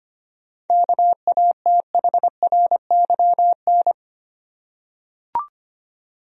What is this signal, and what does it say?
Morse code, heard as a keyed tone, about 25 wpm, 700 Hz. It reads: KATHRYN
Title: USA Common Names - 25wpm